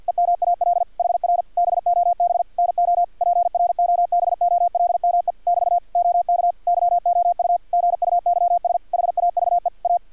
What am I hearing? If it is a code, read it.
Morse code, heard as a keyed tone, about 50 wpm, 700 Hz, with radio noise. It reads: EMAL HR BOB NO PROBOLME <BT> OK YOU GUYS HAVE A
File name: archived / CRSnet